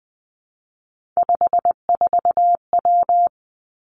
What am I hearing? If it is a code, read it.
Morse code, heard as a keyed tone, about 19 wpm, 700 Hz. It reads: 54W
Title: Three Challenging Letter-Number Combinations 20wpm